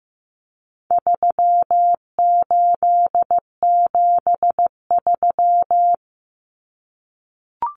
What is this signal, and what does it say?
Morse code, heard as a keyed tone, about 15 wpm, 700 Hz. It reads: 3873
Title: Numbers - 4-Digit 15wpm